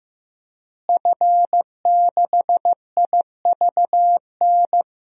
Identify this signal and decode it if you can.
Morse code, heard as a keyed tone, about 16 wpm, 700 Hz. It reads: F6IVN
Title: DXCC-100 Call Signs - 5 Characters - Repeated 3x - 15wpm